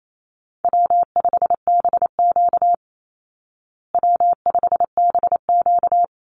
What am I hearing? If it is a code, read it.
Morse code, heard as a keyed tone, about 28 wpm, 700 Hz. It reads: W5BQ W5BQ
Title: Easy Call Signs Repeated 3x - 28wpm